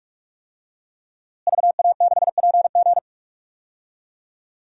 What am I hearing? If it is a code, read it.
Morse code, heard as a keyed tone, about 47 wpm, 700 Hz. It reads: VA6PG
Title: DXCC-100 Call Signs - 5 Characters - Repeated 2x - 45wpm